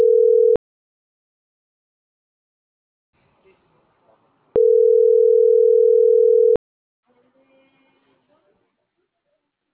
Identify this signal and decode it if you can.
Ringback tone (North America), 440 + 480 Hz.